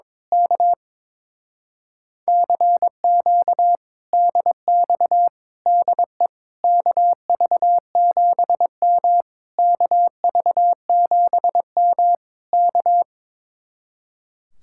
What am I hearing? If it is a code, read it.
Morse code, heard as a keyed tone, about 22 wpm, 700 Hz. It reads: K CQ DX DE KV7M KV7M K
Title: Speed-Racing Common QSO Phrases Encore - 22wpm